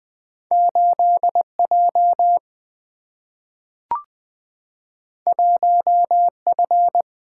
Morse code, heard as a keyed tone, about 20 wpm, 700 Hz. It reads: 8J 1F
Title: Single-Letter Single-Number 20wpm